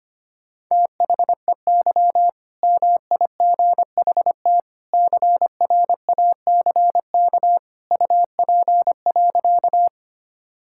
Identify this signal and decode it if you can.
Morse code, heard as a keyed tone, about 25 wpm, 700 Hz. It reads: THEY MIGHT CRACK UP.
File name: practice / 25wpm / All-Age Appropriate Jokes - 25wpm